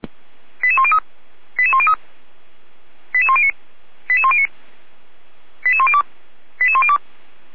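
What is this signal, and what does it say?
Five-tone selective call (ZVEI), heard as quick runs of five tones, tones of about 70 ms. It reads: 88103 88103 88109 88109 88102 88102
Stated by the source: ZVEI five-tone call 88119 received with an SDR.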